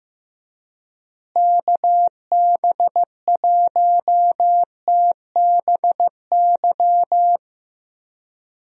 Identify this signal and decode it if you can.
Morse code, heard as a keyed tone, about 15 wpm, 700 Hz. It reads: KB1TBY
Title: USA Call Signs - 6 Characters - Repeated 3x - 15wpm